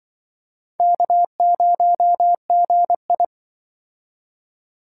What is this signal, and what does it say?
Morse code, heard as a keyed tone, about 24 wpm, 700 Hz. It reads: K0GI
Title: Speed Racing - USA 1x2 Call Signs 20wpm